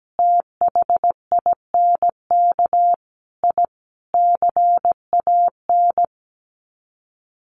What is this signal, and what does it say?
Morse code, heard as a keyed tone, about 17 wpm, 700 Hz. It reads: THINK I CAN
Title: Sets of 5 Words Encore2 17wpm